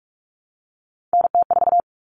Morse code, heard as a keyed tone, about 44 wpm, 700 Hz. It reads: NT4
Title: All Cut Numbers - 3-Digit 45wpm